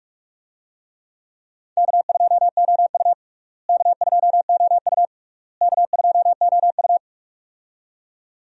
Morse code, heard as a keyed tone, about 42 wpm, 700 Hz. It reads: K2OU K2OU K2OU
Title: USA Call Signs - 4 Characters - Repeated 3x - 45wpm